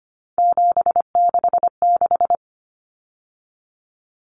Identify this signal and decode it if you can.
Morse code, heard as a keyed tone, about 25 wpm, 690 Hz. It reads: 766
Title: Numbers - 3-Digit 25wpm